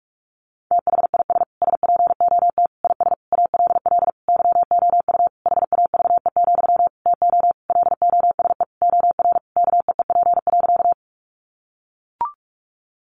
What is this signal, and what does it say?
Morse code, heard as a keyed tone, about 44 wpm, 700 Hz. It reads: THIS SPOT IS ALL YOU HAVE, TO LOSE OR KEEP.
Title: Sentences from Top 1100 Words 45wpm